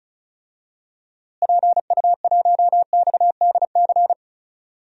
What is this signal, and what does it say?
Morse code, heard as a keyed tone, about 34 wpm, 700 Hz. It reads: PU1XDC
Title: DXCC-100 Call Signs - 6 Characters - Repeated 2x - 35wpm